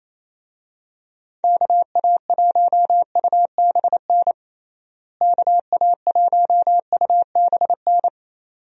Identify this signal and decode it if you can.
Morse code, heard as a keyed tone, about 27 wpm, 700 Hz. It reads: KA1UBN KA1UBN
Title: USA Call Signs - 6 Characters - Repeated 2x - 28wpm